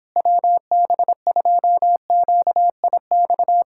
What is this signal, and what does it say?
Morse code, heard as a keyed tone, about 26 wpm, 700 Hz. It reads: WB2QIX
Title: Speed-Racing Common QSO Phrases - 22wpm